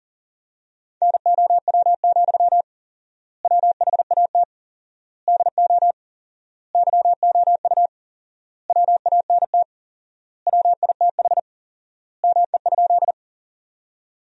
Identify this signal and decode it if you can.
Morse code, heard as a keyed tone, about 37 wpm, 700 Hz. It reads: NOW, WHAT DO YOU WANT WITH ME?
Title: 4x Word-Spacing - Sentences from Top 200 Words - 40wpm